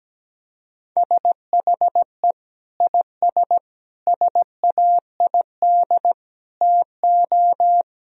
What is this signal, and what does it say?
Morse code, heard as a keyed tone, about 17 wpm, 700 Hz. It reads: SHE IS SAID TO
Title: Sentences from Top 1000 Words - Review Part 2 - 17wpm